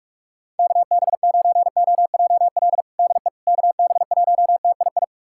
Morse code, heard as a keyed tone, about 42 wpm, 700 Hz. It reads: KB9OJL DE KB1TII
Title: USA Callsign DE Callsign 45wpm